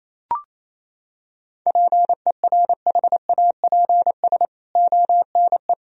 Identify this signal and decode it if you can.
Morse code, heard as a keyed tone, about 28 wpm, 700 Hz. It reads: PERHAPS ONE
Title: Sentences from Top 1000 Words - Review Part 1 - 28wpm